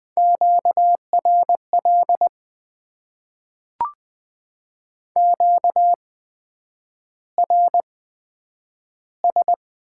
Morse code, heard as a keyed tone, about 20 wpm, spaced as for 9 wpm, 700 Hz. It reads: QRL QRS
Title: Top 90 QSO Elements - 8x Character Spacing - 20wpm